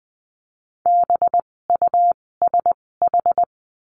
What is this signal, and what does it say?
Morse code, heard as a keyed tone, about 20 wpm, spaced as for 16 wpm, 700 Hz. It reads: BUSH
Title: 256 Lesson 42a - Words 12-20wpm